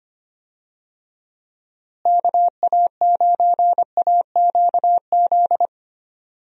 Morse code, heard as a keyed tone, about 25 wpm, 700 Hz. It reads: KA9AQZ
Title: USA Call Signs - 6 Characters - Repeated 3x - 25wpm